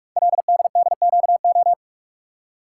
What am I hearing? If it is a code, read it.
Morse code, heard as a keyed tone, about 44 wpm, 690 Hz. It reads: LDDQO
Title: Code Groups - 5 Letters - 45wpm